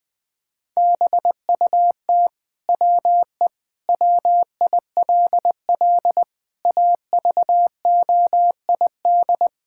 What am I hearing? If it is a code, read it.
Morse code, heard as a keyed tone, about 20 wpm, 700 Hz. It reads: BUT WE WILL AVOID
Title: Sentences from Top 1000 Words - Review Part 6 - 20wpm